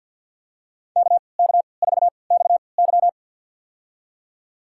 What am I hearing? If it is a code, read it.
Morse code, heard as a keyed tone, about 52 wpm, spaced as for 29 wpm, 700 Hz. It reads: KK4XY
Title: USA Call Signs - 5 Characters - 3x Character Spacing - 50wpm